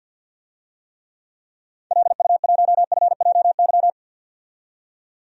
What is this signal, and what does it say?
Morse code, heard as a keyed tone, about 51 wpm, 700 Hz. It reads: LU1FJY
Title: DXCC-100 Call Signs - 6 Characters - 50wpm